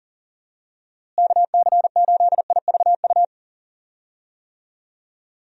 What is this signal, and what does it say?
Morse code, heard as a keyed tone, about 40 wpm, 700 Hz. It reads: KC8IVU
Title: USA Call Signs - 6 Characters - Repeated 3x - 40wpm